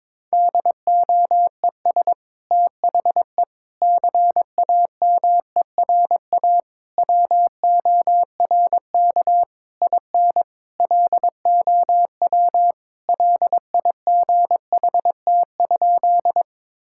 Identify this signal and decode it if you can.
Morse code, heard as a keyed tone, about 22 wpm, 700 Hz. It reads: DOES THE CAMERA WORK IN LOW LIGHT?
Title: Sentences from Top 1000 Words - Review Part 5 - 22wpm